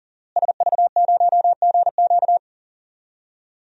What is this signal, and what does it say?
Morse code, heard as a keyed tone, about 40 wpm, 700 Hz. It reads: SV0GQ
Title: DXCC-100 Call Signs - 5 Characters - Repeated 3x - 40wpm